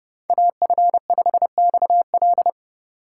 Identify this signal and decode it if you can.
Morse code, heard as a keyed tone, about 31 wpm, 700 Hz. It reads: AF5XL